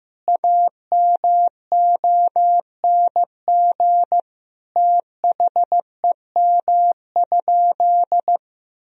Morse code, heard as a keyed tone, about 15 wpm, 700 Hz. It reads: AMONG THEM?
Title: Sentences from Top 1000 Words - Review Part 3 - 15wpm